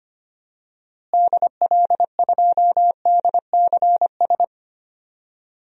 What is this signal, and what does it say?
Morse code, heard as a keyed tone, about 25 wpm, 700 Hz. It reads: DL2DCS
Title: DXCC-100 Call Signs - 6 Characters - Repeated 3x - 25wpm